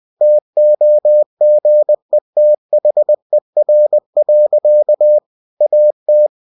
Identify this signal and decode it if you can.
Morse code, heard as a keyed tone, about 20 wpm, 600 Hz. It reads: TOGETHER. AT
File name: cw-books / winnie-the-pooh / winnie-20wpm-page-0034